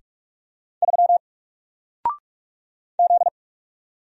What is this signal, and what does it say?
Morse code, heard as a keyed tone, about 44 wpm, 700 Hz. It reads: 3 Z
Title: Single Character (Letter:Number:Punctuation:Prosign) 45wpm